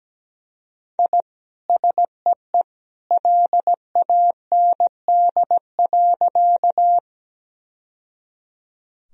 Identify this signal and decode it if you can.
Morse code, heard as a keyed tone, about 17 wpm, 700 Hz. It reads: I SEE LAND.
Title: Sentences from Top 1000 Words - Review Part 1 - 17wpm